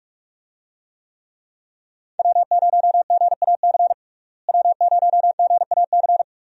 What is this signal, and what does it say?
Morse code, heard as a keyed tone, about 41 wpm, 700 Hz. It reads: W0GAC W0GAC
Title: USA Call Signs - 5 Characters - Repeated 2x - 45wpm